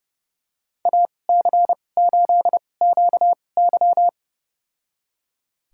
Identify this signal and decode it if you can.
Morse code, heard as a keyed tone, about 30 wpm, spaced as for 22 wpm, 700 Hz. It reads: AC8QY
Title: USA Call Signs - 2x Character Spacing - 30wpm